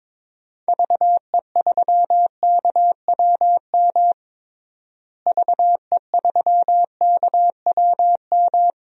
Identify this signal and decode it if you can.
Morse code, heard as a keyed tone, about 22 wpm, 700 Hz. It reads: VE3KWM VE3KWM
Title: DXCC-100 Call Signs - 6 Characters - Repeated 3x - 22wpm